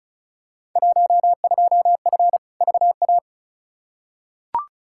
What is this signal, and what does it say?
Morse code, heard as a keyed tone, about 36 wpm, 700 Hz. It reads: 12F VA
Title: Field Day Exchanges Encore - 35wpm